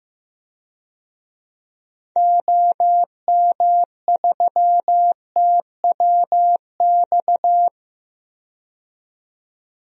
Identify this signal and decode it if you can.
Morse code, heard as a keyed tone, about 15 wpm, 700 Hz. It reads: OM3TWX
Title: DXCC-100 Call Signs - 6 Characters - 15wpm